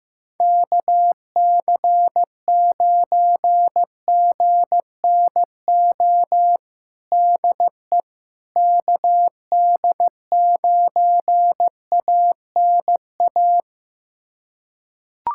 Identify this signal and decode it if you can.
Morse code, heard as a keyed tone, about 15 wpm, 700 Hz. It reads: KC9GNO DE KD9ANA
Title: USA Callsign DE Callsign - 6 Characters - 15wpm